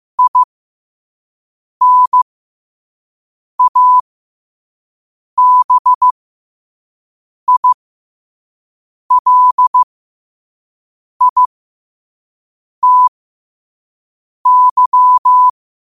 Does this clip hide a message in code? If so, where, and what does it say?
Yes: Morse code from 0.2 to 15.5 s, heard as a keyed tone, about 15 wpm, spaced as for 5 wpm, 1000 Hz. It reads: INABILITY